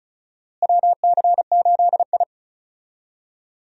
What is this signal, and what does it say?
Morse code, heard as a keyed tone, about 36 wpm, 700 Hz. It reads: WC8I
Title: USA Call Signs - 4 Characters - Repeated 2x - 35wpm